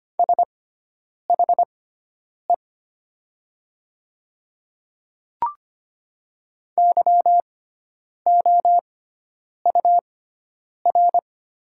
Morse code, heard as a keyed tone, about 25 wpm, spaced as for 9 wpm, 700 Hz. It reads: SHE YOUR
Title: 6x Character Spacing - Top 100 Words - 25wpm